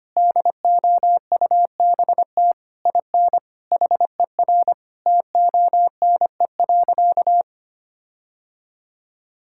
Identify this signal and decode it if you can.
Morse code, heard as a keyed tone, about 25 wpm, 700 Hz. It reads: DOUBT IN HER TONE.
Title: Sentences from Top 1600 Words 25wpm